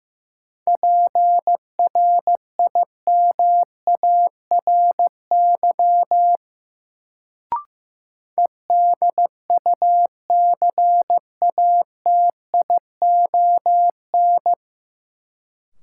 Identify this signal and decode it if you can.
Morse code, heard as a keyed tone, about 15 wpm, 700 Hz. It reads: PRIMARY EDUCATION
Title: Sentences from Top 1100 Words - New Word Drill Alt - 15wpm